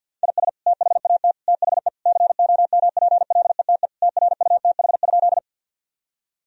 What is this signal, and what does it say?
Morse code, heard as a keyed tone, about 47 wpm, 700 Hz. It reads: IS THAT THE COMPLETE TRUTH?
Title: Sentences from Top 1000 Words - Review Part 2 - 50wpm